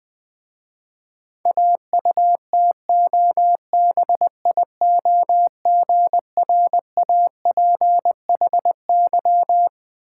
Morse code, heard as a keyed tone, about 21 wpm, 700 Hz. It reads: AUTOBIOGRAPHY
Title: Difficult Words with Hints - 20wpm